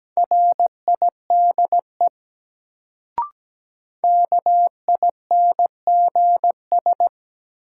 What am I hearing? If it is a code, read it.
Morse code, heard as a keyed tone, about 17 wpm, 700 Hz. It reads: RIDE KINGS
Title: Sentences from Top 1000 Words - Review Part 6 - New Word Drill - 17wpm